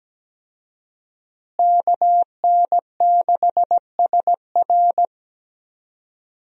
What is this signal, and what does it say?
Morse code, heard as a keyed tone, about 17 wpm, 700 Hz. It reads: KN6SR
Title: USA Call Signs - 5 Characters - Repeated 2x - 17wpm